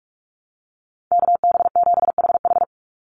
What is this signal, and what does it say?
Morse code, heard as a keyed tone, about 44 wpm, 700 Hz. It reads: KB7HH
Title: NA SOTA Chaser Callsigns 45wpm